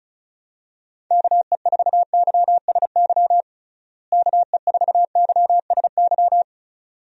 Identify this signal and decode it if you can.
Morse code, heard as a keyed tone, about 36 wpm, 700 Hz. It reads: KE4YSY KE4YSY
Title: USA Call Signs - 6 Characters - Repeated 2x - 35wpm